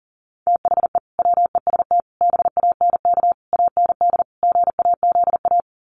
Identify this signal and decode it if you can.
Morse code, heard as a keyed tone, about 40 wpm, 700 Hz. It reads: THE WEST BANK AND GAZA
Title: Sets of 5 Words 40wpm